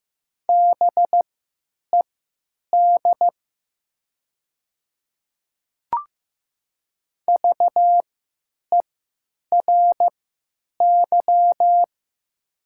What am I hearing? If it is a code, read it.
Morse code, heard as a keyed tone, about 15 wpm, spaced as for 9 wpm, 700 Hz. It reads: BED VERY
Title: 3x Character Spacing - Top 500 Words - 15wpm